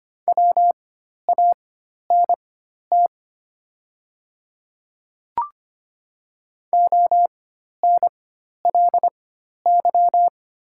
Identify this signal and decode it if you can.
Morse code, heard as a keyed tone, about 25 wpm, spaced as for 12 wpm, 690 Hz. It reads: WANT ONLY
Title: 4x Character Spacing - Top 200 Words - 25wpm